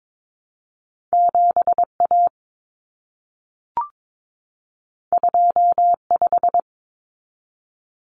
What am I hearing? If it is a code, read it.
Morse code, heard as a keyed tone, about 22 wpm, 700 Hz. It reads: 7A 25
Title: Common Cut Numbers - 2-Digit 22wpm